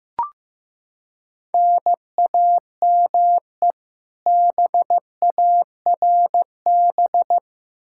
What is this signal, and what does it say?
Morse code, heard as a keyed tone, about 15 wpm, 700 Hz. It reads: NAME BARB
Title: K1USN SST QSO Practice 15wpm